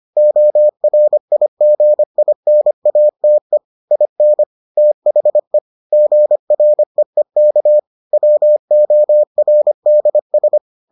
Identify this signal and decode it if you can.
Morse code, heard as a keyed tone, about 25 wpm, 600 Hz. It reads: ORIGINATE IN THE GREEK WORDS